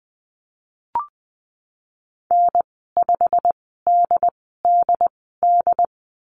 Morse code, heard as a keyed tone, about 20 wpm, spaced as for 15 wpm, 700 Hz. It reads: N5DDD
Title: 154 Lesson 25g - Letter p - Call Signs 20wpm